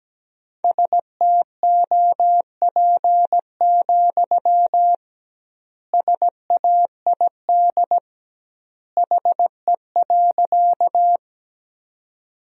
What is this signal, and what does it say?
Morse code, heard as a keyed tone, about 17 wpm, 700 Hz. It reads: STOP, SAID HE.
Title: 2x Word-Spacing - Sentences from Top 300 Words - 17wpm